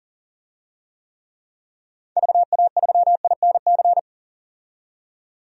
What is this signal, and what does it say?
Morse code, heard as a keyed tone, about 40 wpm, 700 Hz. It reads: VA3INC
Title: DXCC-100 Call Signs - 6 Characters - 40wpm